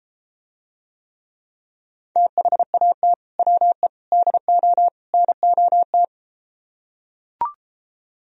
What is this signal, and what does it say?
Morse code, heard as a keyed tone, about 33 wpm, 700 Hz. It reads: THAT WE DO NOT